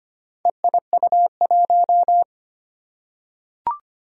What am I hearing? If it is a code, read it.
Morse code, heard as a keyed tone, about 25 wpm, 700 Hz. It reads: EIU1
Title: Four Challenging Letter-Number Combinations 25wpm